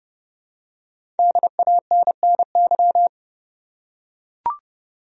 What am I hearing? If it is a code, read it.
Morse code, heard as a keyed tone, about 30 wpm, 700 Hz. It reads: DANNY
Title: USA Common Names - Max 5 Letters - 30wpm